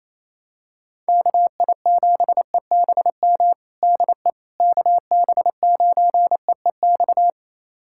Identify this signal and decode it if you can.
Morse code, heard as a keyed tone, about 28 wpm, 700 Hz. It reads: KI7EBM DE KB9EEX
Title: USA Callsign DE Callsign - 6 Characters - 28wpm